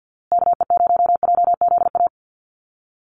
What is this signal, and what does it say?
Morse code, heard as a keyed tone, about 50 wpm, 700 Hz. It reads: KE0JZA
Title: USA Call Signs Repeated 2x 50wpm